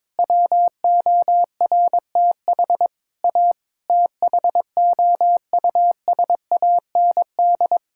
Morse code, heard as a keyed tone, about 22 wpm, 700 Hz. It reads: WORTH A THOUSAND
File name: practice / 22wpm / Sets of 7 Words - 22wpm